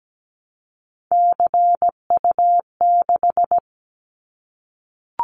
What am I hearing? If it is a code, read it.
Morse code, heard as a keyed tone, about 17 wpm, 700 Hz. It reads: CU6
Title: Callsign Prefixes for Top 100 DXCC - 17wpm